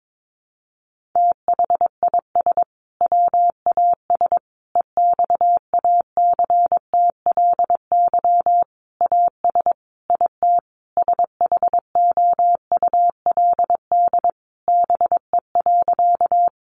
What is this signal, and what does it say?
Morse code, heard as a keyed tone, about 22 wpm, 700 Hz. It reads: THIS WAS EXACTLY AS IT SHOULD BE.
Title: Sentences from Top 900 Words 22wpm